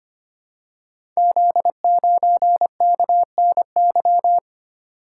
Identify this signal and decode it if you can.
Morse code, heard as a keyed tone, about 25 wpm, 700 Hz. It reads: Z9KNY
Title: Code Groups - 5 Letter:Numbers - 25wpm